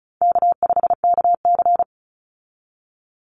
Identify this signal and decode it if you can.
Morse code, heard as a keyed tone, about 34 wpm, 700 Hz. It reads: K5KC
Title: Easy Call Signs Repeated 3x - 35wpm